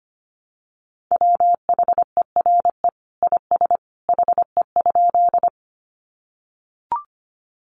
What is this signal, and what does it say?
Morse code, heard as a keyed tone, about 25 wpm, 700 Hz. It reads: WHERE IS HE?
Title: Sentences from Top 200 Words 25wpm